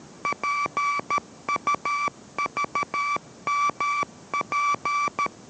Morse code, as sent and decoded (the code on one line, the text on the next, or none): .--. ..- ...- -- .--.
PUVMP